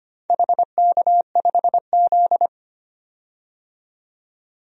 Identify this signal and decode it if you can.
Morse code, heard as a keyed tone, about 25 wpm, 700 Hz. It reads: HK5Z
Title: DXCC-100 Call Signs - 4 Characters - 25wpm